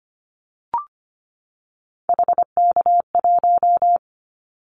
Morse code, heard as a keyed tone, about 25 wpm, 700 Hz. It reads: HK1
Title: Callsign Prefixes for Top 100 DXCC - 25wpm